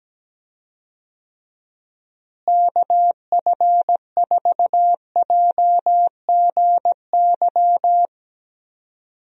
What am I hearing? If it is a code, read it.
Morse code, heard as a keyed tone, about 17 wpm, 700 Hz. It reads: KF4JGY
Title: USA Call Signs - 6 Characters - Repeated 2x - 17wpm